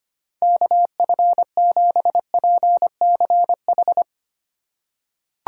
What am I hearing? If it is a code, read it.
Morse code, heard as a keyed tone, about 25 wpm, 700 Hz. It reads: KF7PCH
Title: USA Call Signs - 6 Characters - 4x Character Spacing - 25wpm